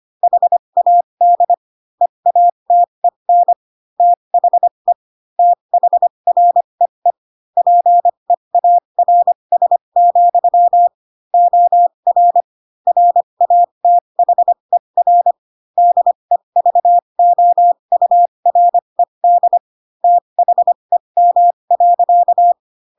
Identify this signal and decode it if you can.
Morse code, heard as a keyed tone, about 25 wpm, spaced as for 22 wpm, 700 Hz. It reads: HAD EATEN THE THREE PEARS, OR RATHER DEVOURED THEM.
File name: Pinocchio / Pinocchio ch7 18 25 ik1hge